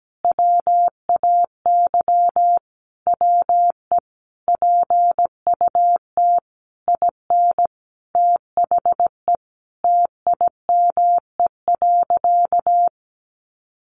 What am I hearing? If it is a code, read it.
Morse code, heard as a keyed tone, about 17 wpm, 700 Hz. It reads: WAY WE PUT IN THE TIME.
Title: Sentences from Top 200 Words 17wpm